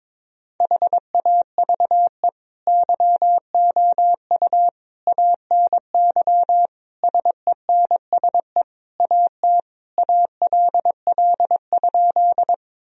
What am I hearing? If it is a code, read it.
Morse code, heard as a keyed tone, about 21 wpm, 700 Hz. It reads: HAVE YOU ANY SENSE AT ALL?
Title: Sentences from Top 1000 Words - Review Part 4 - 22wpm